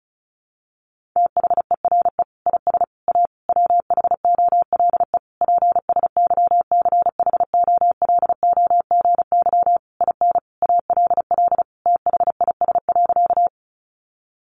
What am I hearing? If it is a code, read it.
Morse code, heard as a keyed tone, about 34 wpm, 700 Hz. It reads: THERE IS A WHOLE PSYCHOLOGY IN ALL THIS.
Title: Sentences from Top 1400 Words 35wpm